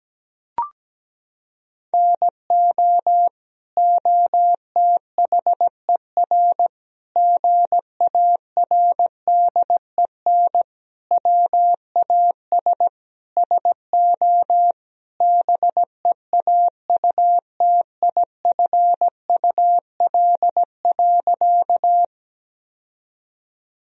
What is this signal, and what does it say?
Morse code, heard as a keyed tone, about 17 wpm, 700 Hz. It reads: NO OTHER GARDEN WAS SO BEAUTIFUL.
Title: Sentences from Top 1000 Words - Review Part 4 - 17wpm